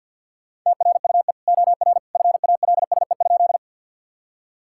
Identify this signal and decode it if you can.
Morse code, heard as a keyed tone, about 50 wpm, 700 Hz. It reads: TRUE OR FALSE?
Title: Sentences from Top 1600 Words 50wpm